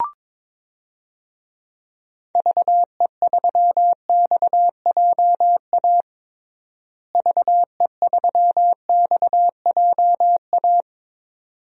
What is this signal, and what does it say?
Morse code, heard as a keyed tone, about 22 wpm, 700 Hz. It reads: VE3XJA VE3XJA
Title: DXCC-100 Call Signs - 6 Characters - Repeated 2x - 22wpm